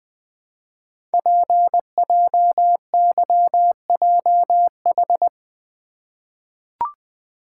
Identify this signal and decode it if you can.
Morse code, heard as a keyed tone, about 20 wpm, 700 Hz. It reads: PJYJH
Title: Code Groups - 5 Letters - 20wpm